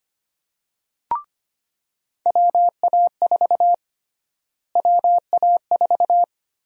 Morse code, heard as a keyed tone, about 25 wpm, 700 Hz. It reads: WA4 WA4
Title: USA Call Sign Prefixes - Repeated 2x - 25wpm